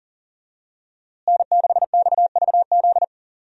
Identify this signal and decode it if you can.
Morse code, heard as a keyed tone, about 38 wpm, 700 Hz. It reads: N6XVZ
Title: USA Call Signs - 5 Characters - Repeated 3x - 40wpm